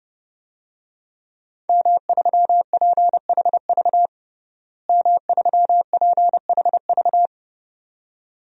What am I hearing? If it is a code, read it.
Morse code, heard as a keyed tone, about 30 wpm, 700 Hz. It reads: M3PHV M3PHV
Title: DXCC-100 Call Signs - 5 Characters - Repeated 2x - 30wpm